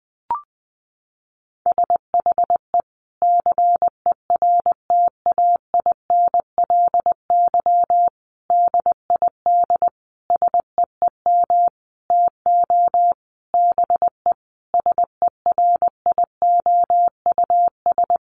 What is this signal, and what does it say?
Morse code, heard as a keyed tone, about 20 wpm, 700 Hz. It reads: SHE CERTAINLY DID SEEM TO BE SERIOUS
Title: Sentences from Top 1100 Words 20wpm